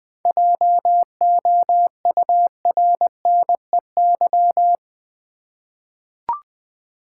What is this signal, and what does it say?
Morse code, heard as a keyed tone, about 21 wpm, 700 Hz. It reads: JOURNEY
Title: 7 Letter Words - First 3 Letters Given - 20wpm